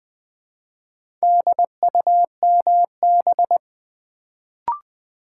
Morse code, heard as a keyed tone, about 20 wpm, 700 Hz. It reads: DUMB
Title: Build A Word - 1 Letter At A Time - 4-Letter Words - 20wpm